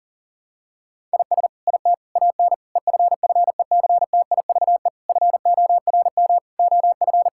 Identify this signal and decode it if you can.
Morse code, heard as a keyed tone, about 40 wpm, 700 Hz. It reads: IS IT AN EFFECTIVE FORM OF
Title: Sentences from Top 1000 Words - Review Part 6 - 40wpm